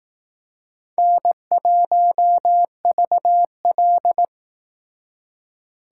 Morse code, heard as a keyed tone, about 18 wpm, 700 Hz. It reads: N1VL
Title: Speed Racing - USA 1x2 Call Signs 15wpm